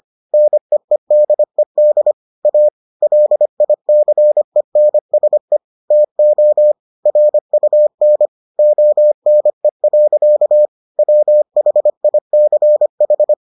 Morse code, heard as a keyed tone, about 25 wpm, 600 Hz. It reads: NEEDED A LICENSE TO RUN ONE. WHICH